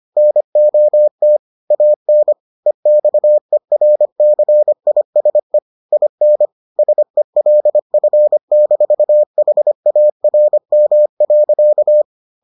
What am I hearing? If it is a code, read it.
Morse code, heard as a keyed tone, about 25 wpm, 600 Hz. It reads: NOT AN EXERCISE IN SELF-HARM.